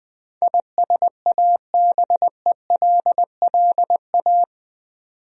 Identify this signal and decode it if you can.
Morse code, heard as a keyed tone, about 20 wpm, 700 Hz. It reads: ISABELLA
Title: USA Common Names - 20wpm